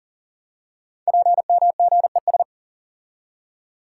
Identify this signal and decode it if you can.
Morse code, heard as a keyed tone, about 42 wpm, 690 Hz. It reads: PMGES
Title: Code Groups - 5 Letters - 40wpm